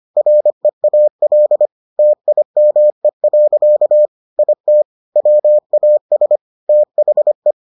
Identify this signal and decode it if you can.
Morse code, heard as a keyed tone, about 25 wpm, 600 Hz. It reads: REAL TIME. IT WAS THE